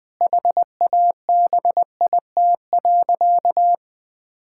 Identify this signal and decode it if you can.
Morse code, heard as a keyed tone, about 21 wpm, 700 Hz. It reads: HABIT.
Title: Difficult Sentences with Hints - 20wpm